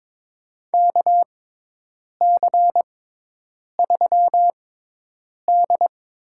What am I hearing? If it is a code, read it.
Morse code, heard as a keyed tone, about 23 wpm, spaced as for 8 wpm, 700 Hz. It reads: KC3D
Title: USA Call Signs - 6 Characters - 6x Character Spacing - 22wpm